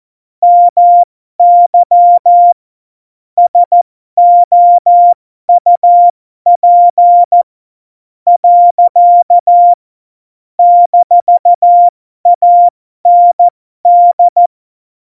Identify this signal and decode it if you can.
Morse code, heard as a keyed tone, about 14 wpm, spaced as for 11 wpm, 700 Hz. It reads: MY SOUP. - AND
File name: Morsekurs / Witze